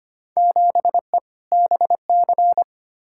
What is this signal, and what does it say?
Morse code, heard as a keyed tone, about 25 wpm, 700 Hz. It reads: 7E BC
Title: Field Day Exchanges Encore - 25wpm